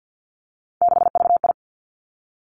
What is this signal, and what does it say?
Morse code, heard as a keyed tone, about 50 wpm, 700 Hz. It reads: 6VI
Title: Three Challenging Letter-Number Combinations 50wpm